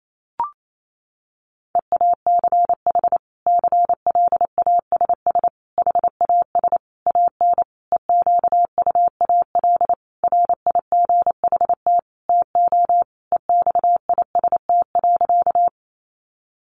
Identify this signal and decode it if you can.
Morse code, heard as a keyed tone, about 28 wpm, 700 Hz. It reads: EACH CLASS HAS AN EQUAL RIGHT TO EXIST.
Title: Sentences from Top 1400 Words 28wpm